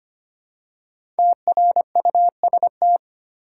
Morse code, heard as a keyed tone, about 25 wpm, 700 Hz. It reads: TRUST
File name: 5 Letter Words - First 3 Letters Given - 25wpm